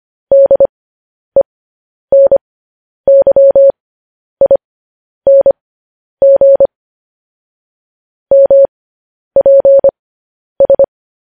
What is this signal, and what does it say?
Morse code, heard as a keyed tone, about 25 wpm, spaced as for 10 wpm, 550 Hz. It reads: DENYING MPS
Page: news headlines in morse code at 10 wpm